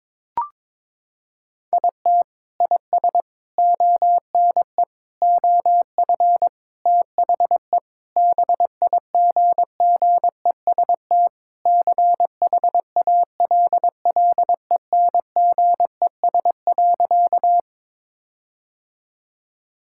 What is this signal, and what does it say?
Morse code, heard as a keyed tone, about 22 wpm, 700 Hz. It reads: IT IS ONE OF THE BIGGEST CHALLENGES.
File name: practice / 22wpm / Sentences with 4-Word Phrases - 22wpm